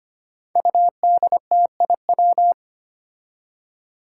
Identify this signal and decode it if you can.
Morse code, heard as a keyed tone, about 25 wpm, 700 Hz. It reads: UDTIW
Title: Five Challenging Letter-Number Combinations 25wpm